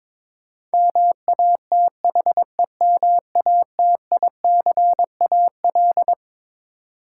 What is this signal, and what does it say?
Morse code, heard as a keyed tone, about 21 wpm, 700 Hz. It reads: MATHEMATICAL
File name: Twelve Letter Words - 22wpm